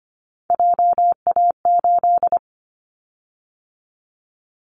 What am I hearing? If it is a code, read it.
Morse code, heard as a keyed tone, about 25 wpm, 700 Hz. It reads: JA8
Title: Callsign Prefixes for Top 25 DXCC - 25wpm